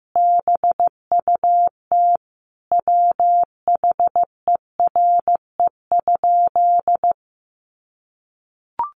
Morse code, heard as a keyed tone, about 15 wpm, 700 Hz. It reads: BUT WHERE?
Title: Sentences from Top 1200 Words 15wpm